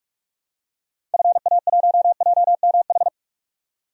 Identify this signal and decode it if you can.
Morse code, heard as a keyed tone, about 42 wpm, 700 Hz. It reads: FA1JMH
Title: DXCC-100 Call Signs - 6 Characters - 45wpm